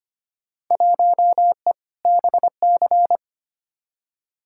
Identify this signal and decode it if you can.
Morse code, heard as a keyed tone, about 25 wpm, 700 Hz. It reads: 1E BC
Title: Field Day Exchanges - 25wpm